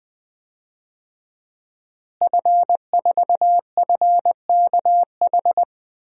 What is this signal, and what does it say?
Morse code, heard as a keyed tone, about 20 wpm, 700 Hz. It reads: F4FKH
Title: DXCC-100 Call Signs - 5 Characters - Repeated 3x - 20wpm